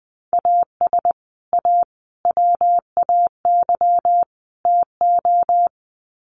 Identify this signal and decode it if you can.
Morse code, heard as a keyed tone, about 20 wpm, 700 Hz. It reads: AS A WAY TO
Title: Sets of 4 Words 20wpm